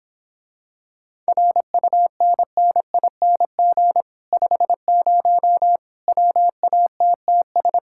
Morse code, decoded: RUNNING 50 WATTS

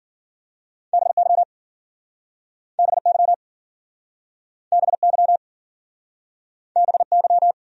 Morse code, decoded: BY BY BY BY

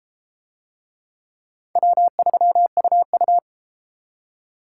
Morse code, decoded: W3UU

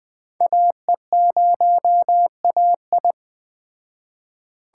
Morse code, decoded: AE0AI

